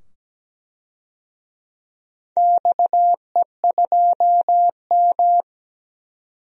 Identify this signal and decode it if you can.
Morse code, heard as a keyed tone, about 17 wpm, 700 Hz. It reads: XE2M
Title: DXCC-100 Call Signs - 4 Characters - 17wpm